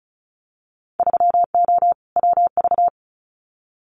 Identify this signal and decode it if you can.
Morse code, heard as a keyed tone, about 36 wpm, 690 Hz. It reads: 3O WV
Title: Winter Field Day Exchanges 35wpm